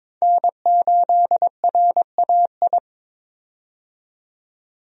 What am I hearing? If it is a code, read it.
Morse code, heard as a keyed tone, about 22 wpm, 700 Hz. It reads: N8RAI
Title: USA Call Signs - 5 Characters - 22wpm